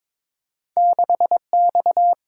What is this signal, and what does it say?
Morse code, heard as a keyed tone, about 30 wpm, spaced as for 26 wpm, 700 Hz. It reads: 6X
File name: practice / 22wpm / Speed-Racing - Two Letter-Number Combinations - 22wpm